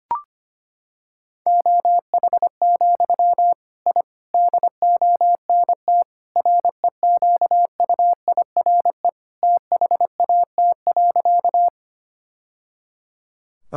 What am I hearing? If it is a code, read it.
Morse code, heard as a keyed tone, about 25 wpm, 700 Hz. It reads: OH, I DONT REQUIRE THAT.
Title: Sentences from Top 1000 Words - Review Part 6 - 25wpm